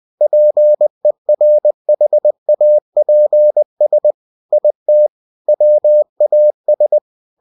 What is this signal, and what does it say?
Morse code, heard as a keyed tone, about 20 wpm, 600 Hz. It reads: PERHAPS IT WAS
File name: cw-books / winnie-the-pooh / winnie-20wpm-page-0089